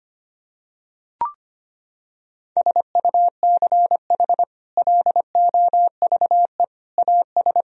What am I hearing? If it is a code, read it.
Morse code, heard as a keyed tone, about 25 wpm, 700 Hz. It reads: SUCH LOVE AS
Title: Sentences from Top 1000 Words - Review Part 1 - 25wpm